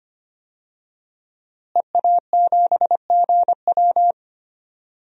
Morse code, heard as a keyed tone, about 25 wpm, 700 Hz. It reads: EA7GW
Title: DXCC-100 Call Signs - 5 Characters - Repeated 2x - 25wpm